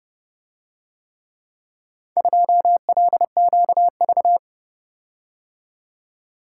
Morse code, heard as a keyed tone, about 29 wpm, 700 Hz. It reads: 2LQV